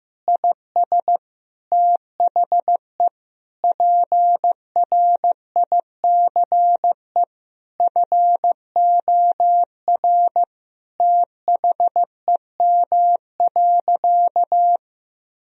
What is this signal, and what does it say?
Morse code, heard as a keyed tone, about 16 wpm, 700 Hz. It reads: IS THE PRICE FOR THEM.
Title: Sentences from Top 1000 Words - Review Part 4 - 15wpm